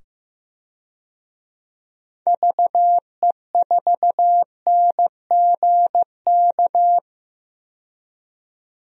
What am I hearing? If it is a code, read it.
Morse code, heard as a keyed tone, about 15 wpm, 700 Hz. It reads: VE4NGK
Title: DXCC-100 Call Signs - 6 Characters - 15wpm